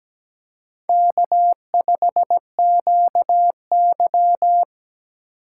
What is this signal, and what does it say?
Morse code, heard as a keyed tone, about 17 wpm, 700 Hz. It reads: K5QY
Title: USA Call Signs - 4 Characters - 4x Character Spacing - 17wpm